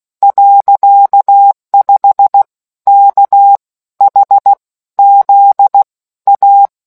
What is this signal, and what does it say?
Morse code, heard as a keyed tone, about 16 wpm, 800 Hz. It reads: .5 K H Z A